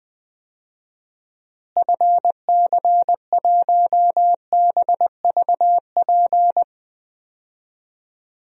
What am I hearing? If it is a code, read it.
Morse code, heard as a keyed tone, about 21 wpm, 700 Hz. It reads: FC1BVP
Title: DXCC-100 Call Signs - 6 Characters - Repeated 3x - 20wpm